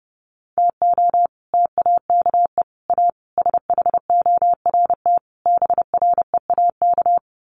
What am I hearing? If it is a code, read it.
Morse code, heard as a keyed tone, about 30 wpm, 700 Hz. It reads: TO TAKE A SHORT BREAK
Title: Sets of 5 Words Encore2 30wpm